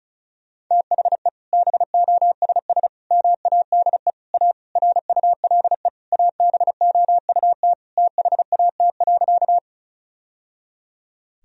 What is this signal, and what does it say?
Morse code, heard as a keyed tone, about 34 wpm, 700 Hz. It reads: THE BOSS MADE A RULE ABOUT THAT.
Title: Sentences from Top 1000 Words - Review Part 4 - 35wpm